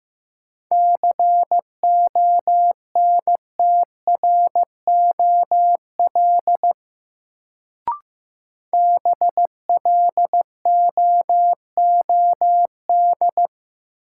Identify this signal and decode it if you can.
Morse code, heard as a keyed tone, about 15 wpm, 700 Hz. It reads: CONTROL BLOOD
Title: Sentences from Top 1000 Words - Review Part 2 - New Word Drill - 15wpm